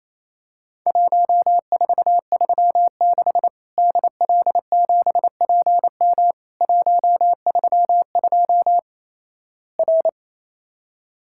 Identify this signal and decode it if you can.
Morse code, heard as a keyed tone, about 28 wpm, 700 Hz. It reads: 1436 DL7PM 132 R
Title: Worked All Europe DX Contest - QTC Exchange - 28wpm